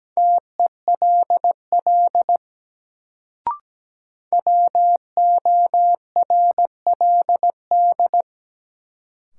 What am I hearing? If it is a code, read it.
Morse code, heard as a keyed tone, about 17 wpm, 700 Hz. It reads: TELL WORLD
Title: Sentences from Top 200 Words - New Word Drill - 17wpm